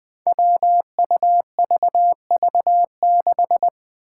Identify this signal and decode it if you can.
Morse code, heard as a keyed tone, about 20 wpm, 700 Hz. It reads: WUVV6
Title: Five Challenging Letter-Number Combinations 20wpm